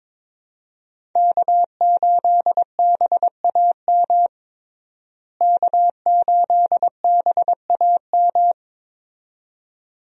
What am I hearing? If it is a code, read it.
Morse code, heard as a keyed tone, about 22 wpm, 700 Hz. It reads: K8BAM K8BAM
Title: USA Call Signs - 5 Characters - Repeated 2x - 22wpm